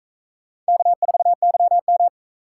Morse code, heard as a keyed tone, about 44 wpm, 700 Hz. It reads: K4YM